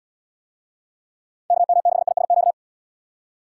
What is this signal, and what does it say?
Morse code, heard as a keyed tone, about 78 wpm, 700 Hz. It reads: KM6ISQ